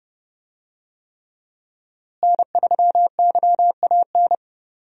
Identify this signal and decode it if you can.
Morse code, heard as a keyed tone, about 32 wpm, 700 Hz. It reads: N3YAN
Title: USA Call Signs - 5 Characters - Repeated 3x - 30wpm